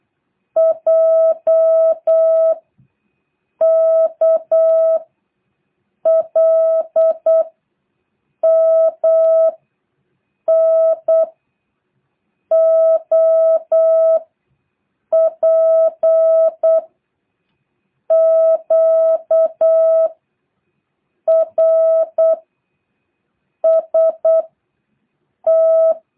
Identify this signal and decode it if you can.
Morse code, heard as a keyed tone, about 8 wpm, 650 Hz. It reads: J K L M N O P Q R S T
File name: jh8cbh / 2023cw